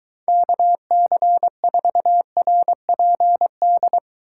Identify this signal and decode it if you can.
Morse code, heard as a keyed tone, about 23 wpm, 700 Hz. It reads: KC4RPD